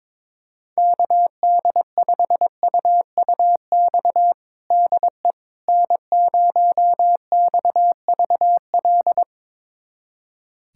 Morse code, heard as a keyed tone, about 21 wpm, 700 Hz. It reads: KD5UUX DE N0XVL
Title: USA Callsign DE Callsign 22wpm